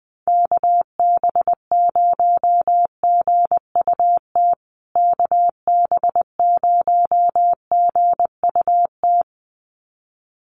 Morse code, heard as a keyed tone, about 20 wpm, 700 Hz. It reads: KB0GUT KB0GUT
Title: USA Call Signs Repeated 2x 20wpm